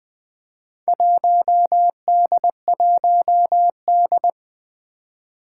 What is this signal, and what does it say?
Morse code, heard as a keyed tone, about 20 wpm, 700 Hz. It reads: 1D1D
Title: Four Challenging Letter-Number Combinations 20wpm